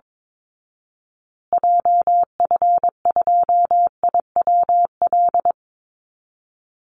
Morse code, heard as a keyed tone, about 22 wpm, 700 Hz. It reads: JF2IWL
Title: Call Signs Repeated 3x - 22wpm